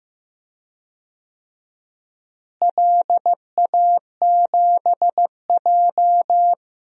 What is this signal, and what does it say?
Morse code, heard as a keyed tone, about 15 wpm, 700 Hz. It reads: LA7J
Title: DXCC-100 Call Signs - 4 Characters - Repeated 3x - 15wpm